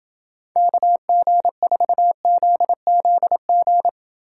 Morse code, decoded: KG4ZZG

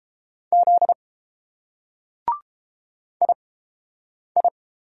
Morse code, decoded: Z II

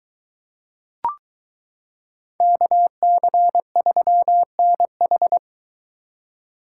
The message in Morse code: KC3NH

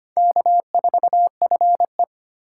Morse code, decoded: K4FE

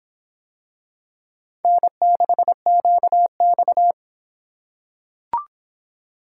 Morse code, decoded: N6QX

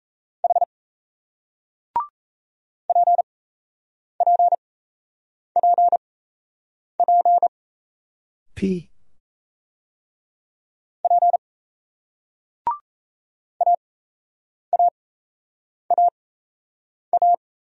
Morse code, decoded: H PPPP P AAAA

